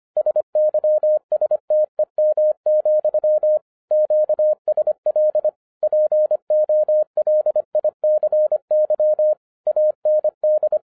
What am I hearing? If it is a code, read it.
Morse code, heard as a keyed tone, about 25 wpm, 610 Hz. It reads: SYSTEM, QSL POLICY AND